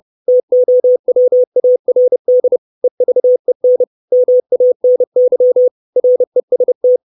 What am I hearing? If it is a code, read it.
Morse code, heard as a keyed tone, about 30 wpm, 500 Hz. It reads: TOWARD EVEN MANY REST
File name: 632 Words 30WPM